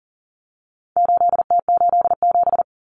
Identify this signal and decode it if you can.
Morse code, heard as a keyed tone, about 40 wpm, 700 Hz. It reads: 8T87
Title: Common Cut Numbers - 4-Digit 40wpm